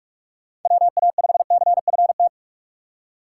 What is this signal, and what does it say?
Morse code, heard as a keyed tone, about 48 wpm, 700 Hz. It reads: WA5CFT